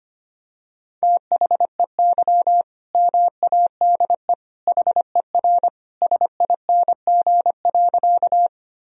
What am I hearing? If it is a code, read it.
Morse code, heard as a keyed tone, about 25 wpm, 700 Hz. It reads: THEY MADE HER SING.
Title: Sentences from Top 1000 Words - Review Part 2 - 25wpm